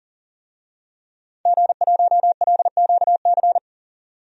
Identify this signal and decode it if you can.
Morse code, heard as a keyed tone, about 40 wpm, 700 Hz. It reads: G1LQC
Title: DXCC-100 Call Signs - 5 Characters - 40wpm